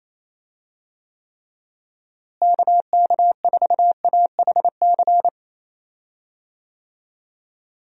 Morse code, heard as a keyed tone, about 28 wpm, 700 Hz. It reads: KK4AHC
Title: USA Call Signs - 6 Characters - Repeated 3x - 28wpm